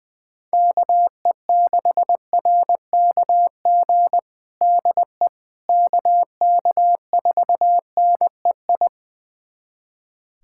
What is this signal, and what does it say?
Morse code, heard as a keyed tone, about 20 wpm, 700 Hz. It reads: KE6RKG DE KK4NEI
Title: USA Callsign DE Callsign 20wpm